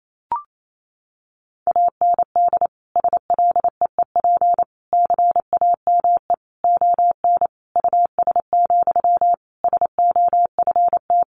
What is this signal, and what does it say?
Morse code, heard as a keyed tone, about 28 wpm, 700 Hz. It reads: AND SLEEP CAME ON US, SOFT
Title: Sentences from Top 1500 Words 28wpm